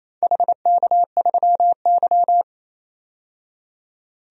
Morse code, heard as a keyed tone, about 28 wpm, 690 Hz. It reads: HK3Y
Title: DXCC-100 Call Signs - 4 Characters - 28wpm